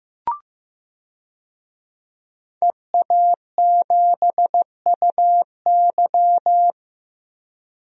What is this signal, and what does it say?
Morse code, heard as a keyed tone, about 15 wpm, 700 Hz. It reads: EA7UY
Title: DXCC-100 Call Signs - 5 Characters - 15wpm